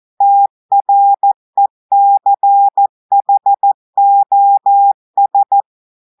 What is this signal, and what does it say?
Morse code, heard as a keyed tone, about 14 wpm, 800 Hz. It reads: TRECHOS
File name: audios / portugais / portugais texte en clair 1 vitesse 14 mots minutes